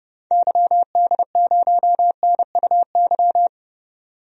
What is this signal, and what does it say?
Morse code, heard as a keyed tone, about 30 wpm, 700 Hz. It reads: YD0NUY
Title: DXCC-100 Call Signs - 6 Characters - 30wpm